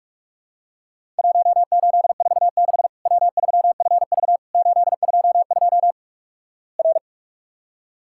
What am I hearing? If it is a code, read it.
Morse code, heard as a keyed tone, about 48 wpm, 700 Hz. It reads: 1846 W3FV 822 R
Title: Worked All Europe DX Contest - QTC Exchange - 45wpm